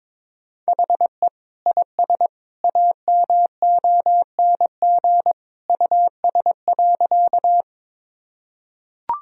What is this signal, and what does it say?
Morse code, heard as a keyed tone, about 21 wpm, 700 Hz. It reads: HE IS AMONG US.
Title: Sentences from Top 1000 Words - Review Part 3 - 22wpm